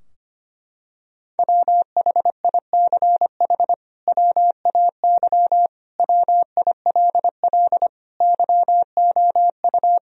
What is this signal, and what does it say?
Morse code, heard as a keyed tone, about 25 wpm, 700 Hz. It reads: WHICH WAY WILL YOU
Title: Sentences from Top 1000 Words - Review Part 2 - 25wpm